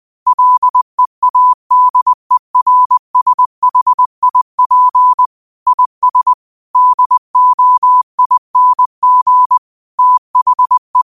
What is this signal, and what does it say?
Morse code, heard as a keyed tone, about 20 wpm, 1000 Hz. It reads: LEADERSHIP IS DOING THE